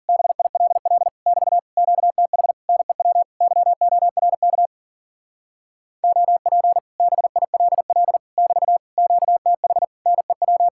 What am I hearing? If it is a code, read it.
Morse code, heard as a keyed tone, about 49 wpm, 700 Hz. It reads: BILL <BT> QTH NEW YORK OP BILL <BT> QTH NEW
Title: Speed-Racing Common QSO Phrases - 40wpm